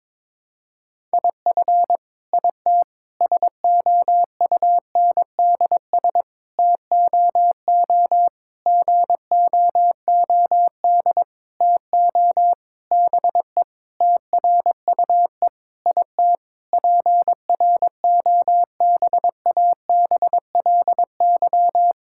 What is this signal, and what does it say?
Morse code, heard as a keyed tone, about 22 wpm, 700 Hz. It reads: IF IT SOUNDS TOO GOOD TO BE TRUE IT PROBABLY
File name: Sentences with 9-Word Phrases - 22wpm